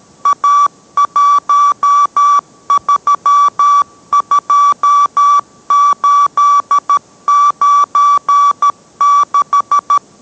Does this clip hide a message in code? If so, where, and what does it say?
Yes: Morse code from 0.3 to 10.0 s, heard as a keyed tone, about 13 wpm, 1150 Hz. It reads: A132896